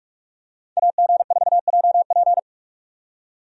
Morse code, heard as a keyed tone, about 41 wpm, 700 Hz. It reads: AG4JP
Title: USA Call Signs - 5 Characters - 45wpm